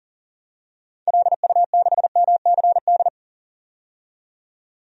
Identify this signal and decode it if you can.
Morse code, heard as a keyed tone, about 40 wpm, 700 Hz. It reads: LU6MCD
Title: DXCC-100 Call Signs - 6 Characters - 40wpm